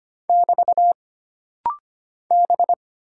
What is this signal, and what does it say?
Morse code, heard as a keyed tone, about 25 wpm, 700 Hz. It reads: <BT> B
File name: Single Character (Letter:Number:Punctuation:Prosign) - Rapid-Fire - 25wpm